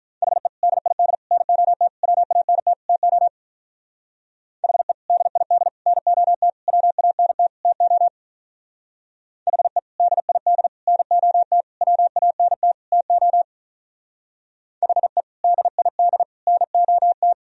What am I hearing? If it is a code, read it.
Morse code, heard as a keyed tone, about 44 wpm, 700 Hz. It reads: HE DID NOT WANT TO HE DID NOT WANT TO HE DID NOT WANT TO HE DID NOT